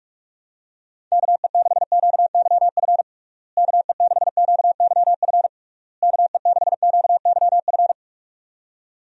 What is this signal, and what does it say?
Morse code, heard as a keyed tone, about 46 wpm, 700 Hz. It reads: KE6QYF KE6QYF KE6QYF
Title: USA Call Signs - 6 Characters - Repeated 3x - 45wpm